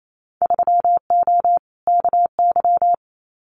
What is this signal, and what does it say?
Morse code, heard as a keyed tone, about 28 wpm, 700 Hz. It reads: 3O KY
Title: Winter Field Day Exchanges 28wpm